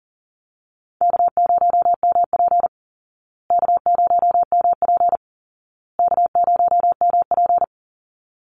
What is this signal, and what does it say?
Morse code, heard as a keyed tone, about 40 wpm, 700 Hz. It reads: K0MP K0MP K0MP
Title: Easy Call Signs Repeated 3x - 40wpm